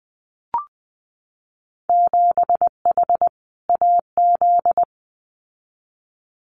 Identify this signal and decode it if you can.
Morse code, heard as a keyed tone, about 20 wpm, 700 Hz. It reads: 7H AZ
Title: Winter Field Day Exchanges Encore 20wpm